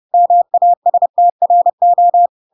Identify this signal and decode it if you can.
Morse code, heard as a keyed tone, about 30 wpm, 700 Hz. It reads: MASTRO
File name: Pinocchio / Pinocchio ch1 30 30 ik1hge